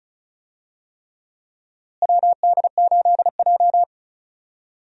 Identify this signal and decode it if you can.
Morse code, heard as a keyed tone, about 34 wpm, 700 Hz. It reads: WD8J
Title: USA Call Signs - 4 Characters - Repeated 3x - 35wpm